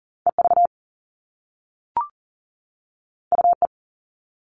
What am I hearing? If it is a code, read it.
Morse code, heard as a keyed tone, about 42 wpm, 700 Hz. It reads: EV UE
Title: All Cut Numbers - 2-Digit 40wpm